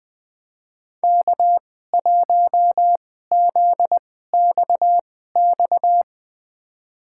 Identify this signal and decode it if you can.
Morse code, heard as a keyed tone, about 20 wpm, spaced as for 14 wpm, 700 Hz. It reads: K1ZXX
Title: USA Call Signs - 5 Characters - 2x Character Spacing - 20wpm